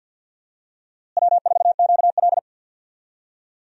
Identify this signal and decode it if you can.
Morse code, heard as a keyed tone, about 51 wpm, 700 Hz. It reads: W4QL
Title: USA Call Signs - 4 Characters - Repeated 3x - 50wpm